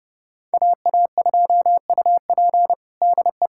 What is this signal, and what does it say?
Morse code, heard as a keyed tone, about 30 wpm, 700 Hz. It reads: AA2UP DE
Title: USA Callsign DE Callsign - 5 Characters - 30wpm